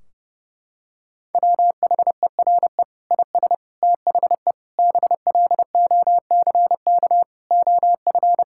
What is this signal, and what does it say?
Morse code, heard as a keyed tone, about 32 wpm, 700 Hz. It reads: WHERE IS THE BLOCK OF
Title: Sentences from Top 1600 Words 30wpm